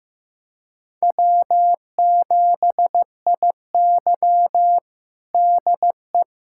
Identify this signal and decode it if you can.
Morse code, heard as a keyed tone, about 15 wpm, 700 Hz. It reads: W7IY DE
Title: USA Callsign DE Callsign - 4 Characters - 15wpm